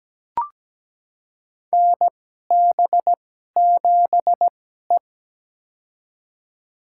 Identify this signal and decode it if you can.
Morse code, heard as a keyed tone, about 17 wpm, spaced as for 12 wpm, 700 Hz. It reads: NB7E
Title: USA Call Signs - 4 Characters - 2x Character Spacing - 17wpm